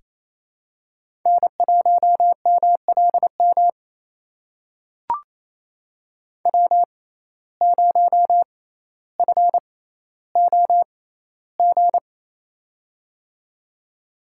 Morse code, heard as a keyed tone, about 28 wpm, spaced as for 14 wpm, 700 Hz. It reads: N1MLM W0FOG
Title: USA Call Signs - 5 Characters - 6x Character Spacing - 28wpm